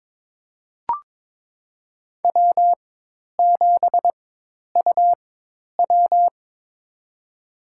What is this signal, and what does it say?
Morse code, heard as a keyed tone, about 22 wpm, spaced as for 10 wpm, 700 Hz. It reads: W7UW
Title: USA Call Signs - 4 Characters - 4x Character Spacing - 22wpm